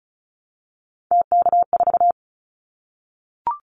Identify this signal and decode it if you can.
Morse code, heard as a keyed tone, about 34 wpm, 700 Hz. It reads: TK4
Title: Callsign Prefixes for Top 100 DXCC - 35wpm